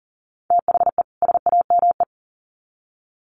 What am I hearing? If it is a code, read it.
Morse code, heard as a keyed tone, about 40 wpm, 700 Hz. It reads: THE SAME
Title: Sets of 2 Words 40wpm